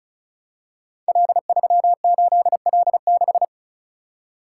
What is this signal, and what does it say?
Morse code, heard as a keyed tone, about 36 wpm, 690 Hz. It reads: L38L6
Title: Code Groups - 5 Letter:Numbers - 35wpm